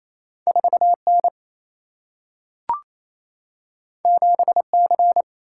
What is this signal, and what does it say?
Morse code, heard as a keyed tone, about 28 wpm, 700 Hz. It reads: 4N 7C
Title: Single-Letter Single-Number 28wpm